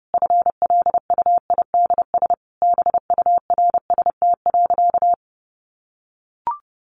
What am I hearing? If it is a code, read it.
Morse code, heard as a keyed tone, about 30 wpm, 700 Hz. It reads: FLUIDS BURST.
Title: A Sound Of Thunder 30wpm